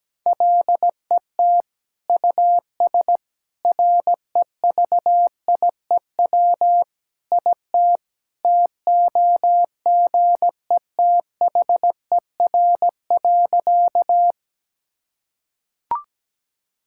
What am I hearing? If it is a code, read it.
Morse code, heard as a keyed tone, about 17 wpm, 700 Hz. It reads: LET US REVIEW IT TOGETHER.
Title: Sentences from Top 1000 Words - Review Part 6 - 17wpm